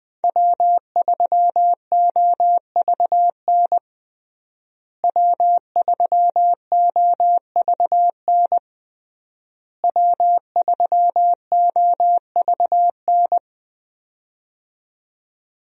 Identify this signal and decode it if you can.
Morse code, heard as a keyed tone, about 21 wpm, 700 Hz. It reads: W3OVN W3OVN W3OVN
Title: USA Call Signs - 5 Characters - Repeated 3x - 20wpm